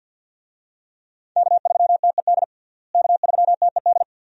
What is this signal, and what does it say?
Morse code, heard as a keyed tone, about 48 wpm, 700 Hz. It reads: K3TED K3TED
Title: USA Call Signs - 5 Characters - Repeated 2x - 50wpm